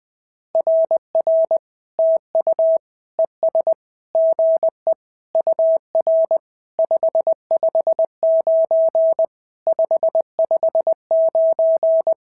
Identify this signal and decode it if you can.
Morse code, heard as a keyed tone, about 19 wpm, 650 Hz. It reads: RR TU ES GE UR 559 559
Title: SOTA QSO Practice 20wpm